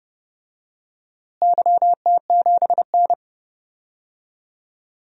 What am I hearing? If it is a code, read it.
Morse code, heard as a keyed tone, about 30 wpm, 700 Hz. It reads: YT7N
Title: DXCC-100 Call Signs - 4 Characters - Repeated 3x - 30wpm